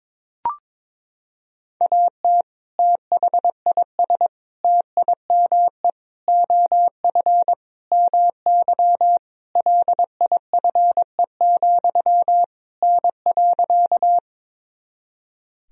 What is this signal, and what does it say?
Morse code, heard as a keyed tone, about 22 wpm, 700 Hz. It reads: AT THIS TIME OF MY LIFE, N.
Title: Sentences from Top 300 Words and Abbreviations 22wpm